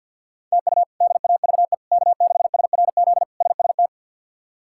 Morse code, heard as a keyed tone, about 52 wpm, 700 Hz. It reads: TU DAVE K6SRZ SST
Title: K1USN SST QSO Practice - 50wpm